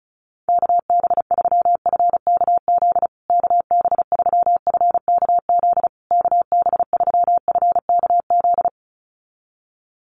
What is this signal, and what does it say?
Morse code, heard as a keyed tone, about 34 wpm, 700 Hz. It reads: KB3FKZ KB3FKZ KB3FKZ
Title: USA Call Signs Repeated 3x 35wpm